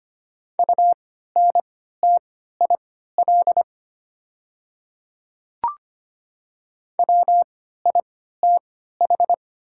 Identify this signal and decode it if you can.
Morse code, heard as a keyed tone, about 25 wpm, spaced as for 14 wpm, 700 Hz. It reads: UNTIL WITH
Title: 3x Character Spacing - Top 300 Words - 25wpm